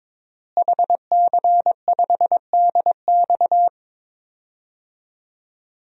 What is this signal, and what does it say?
Morse code, heard as a keyed tone, about 22 wpm, 700 Hz. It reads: HC5DX
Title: DXCC-100 Call Signs - 5 Characters - Repeated 3x - 22wpm